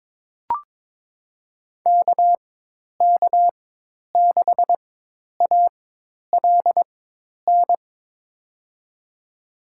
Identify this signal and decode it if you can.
Morse code, heard as a keyed tone, about 22 wpm, spaced as for 10 wpm, 700 Hz. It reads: KK6ALN
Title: USA Call Signs - 6 Characters - 4x Character Spacing - 22wpm